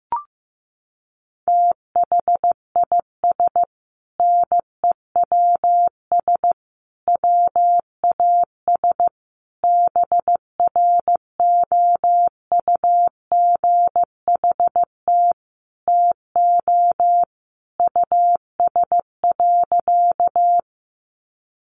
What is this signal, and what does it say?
Morse code, heard as a keyed tone, about 15 wpm, 700 Hz. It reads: THIS NEWS WAS BROUGHT TO US.
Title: Sentences from Top 800 Words 15wpm